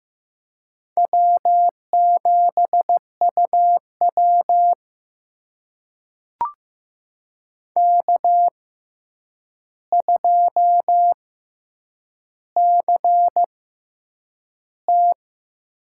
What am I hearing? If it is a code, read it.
Morse code, heard as a keyed tone, about 15 wpm, spaced as for 8 wpm, 700 Hz. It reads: W7UW K2CT
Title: USA Call Signs - 4 Characters - 6x Character Spacing - 15wpm